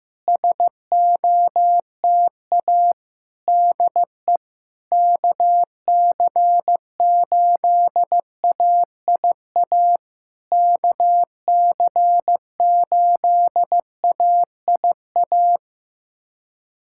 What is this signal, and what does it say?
Morse code, heard as a keyed tone, about 15 wpm, 700 Hz. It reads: SOTA DE KC8AIA KC8AIA
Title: SOTA QSO Practice 15wpm